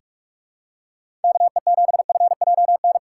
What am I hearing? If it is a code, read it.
Morse code, heard as a keyed tone, about 46 wpm, 700 Hz. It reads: KE7FJN
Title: USA Call Signs - 6 Characters - 6x Character Spacing - 45wpm